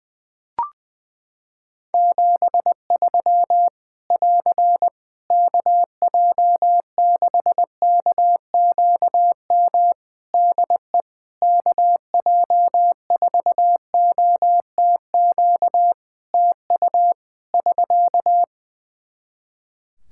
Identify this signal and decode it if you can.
Morse code, heard as a keyed tone, about 20 wpm, 700 Hz. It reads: 73 <AR> KJ6KQM DE KJ4OTQ TU <SK>
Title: Practice QSO - Exchange 4 - Ending - 20wpm